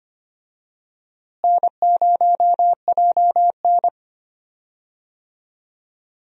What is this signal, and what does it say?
Morse code, heard as a keyed tone, about 25 wpm, 700 Hz. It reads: N0JN
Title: USA Call Signs - 4 Characters - Repeated 2x - 25wpm